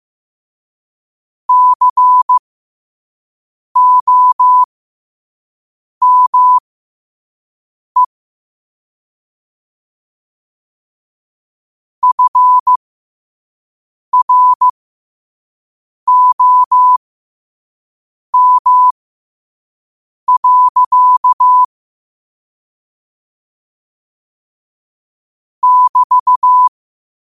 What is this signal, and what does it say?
Morse code, heard as a keyed tone, about 15 wpm, spaced as for 5 wpm, 1000 Hz. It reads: COME FROM. <BT>